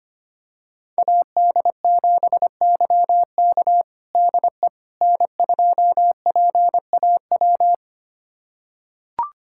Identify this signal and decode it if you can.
Morse code, heard as a keyed tone, about 25 wpm, 700 Hz. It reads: AD7YK DE N2PAW
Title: USA Callsign DE Callsign - 5 Characters - 25wpm